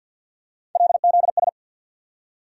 Morse code, heard as a keyed tone, about 50 wpm, 700 Hz. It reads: LZS
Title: USA Call Sign Postfixes - 3 Letters 50wpm